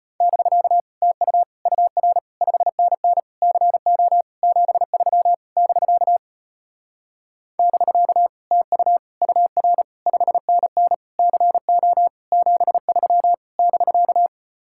Morse code, heard as a keyed tone, about 33 wpm, 700 Hz. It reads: <BK> TU UR 5NN CO 73 <BK> <BK> TU UR 5NN CO 73 <BK>